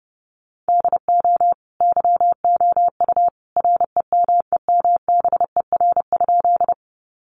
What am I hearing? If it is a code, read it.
Morse code, heard as a keyed tone, about 30 wpm, 700 Hz. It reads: DO YOU REMEMBER?
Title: Sentences from Top 400 Words 30wpm